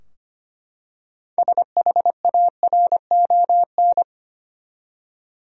Morse code, heard as a keyed tone, about 25 wpm, 700 Hz. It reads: SHARON
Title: USA Common Names - 25wpm